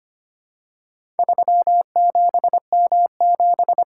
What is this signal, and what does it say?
Morse code, heard as a keyed tone, about 25 wpm, 700 Hz. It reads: 37M7
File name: practice / 25wpm / Four Letter-Number Combinations - 25wpm